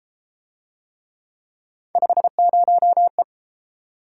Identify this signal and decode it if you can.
Morse code, heard as a keyed tone, about 32 wpm, 700 Hz. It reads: 50E